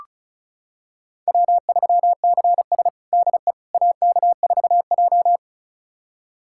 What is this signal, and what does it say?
Morse code, heard as a keyed tone, about 36 wpm, 700 Hz. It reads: W3CS DE AK4J
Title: USA Callsign DE Callsign - 4 Characters - 35wpm